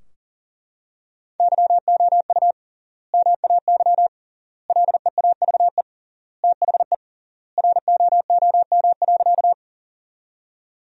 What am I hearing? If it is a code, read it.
Morse code, heard as a keyed tone, about 40 wpm, 700 Hz. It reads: YOU MAY LEAVE THE ROOM.
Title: 3x Word-Spacing - Sentences from Top 400 Words - 40wpm